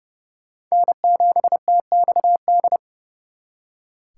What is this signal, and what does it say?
Morse code, heard as a keyed tone, about 40 wpm, 700 Hz. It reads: N7TXD